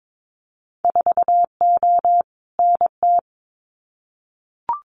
Morse code, heard as a keyed tone, about 22 wpm, 700 Hz. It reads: 4O NT
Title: Winter Field Day Exchanges Encore 22wpm